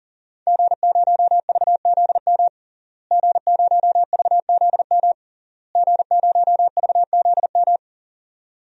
Morse code, heard as a keyed tone, about 40 wpm, 700 Hz. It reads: G0VZM G0VZM G0VZM
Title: DXCC-100 Call Signs - 5 Characters - Repeated 3x - 40wpm